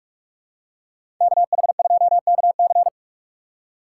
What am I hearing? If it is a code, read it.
Morse code, heard as a keyed tone, about 42 wpm, 700 Hz. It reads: KH2KC
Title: USA Call Signs - 5 Characters - 4x Character Spacing - 45wpm